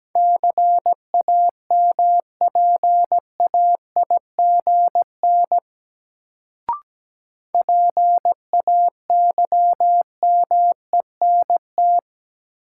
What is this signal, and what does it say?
Morse code, heard as a keyed tone, about 17 wpm, 700 Hz. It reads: CAMPAIGN PAYMENT
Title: Sentences from Top 1200 Words - New Word Drill - 17wpm